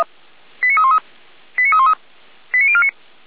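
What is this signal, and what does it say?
Five-tone selective call (ZVEI), heard as quick runs of five tones, tones of about 70 ms. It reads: 89313 89313 89049
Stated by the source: ZVEI five-tone call received by SDR, from Feuerwehr Orlamünde recording server.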